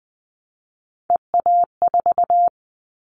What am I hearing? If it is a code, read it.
Morse code, heard as a keyed tone, about 20 wpm, 700 Hz. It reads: EA4
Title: Callsign Prefixes for Top 25 DXCC - Repeated 3x - 20wpm